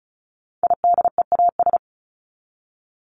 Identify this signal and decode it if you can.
Morse code, heard as a keyed tone, about 34 wpm, 700 Hz. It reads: IDEAS
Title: ES:S Suffix - Top 100 Words with Base Words 35wpm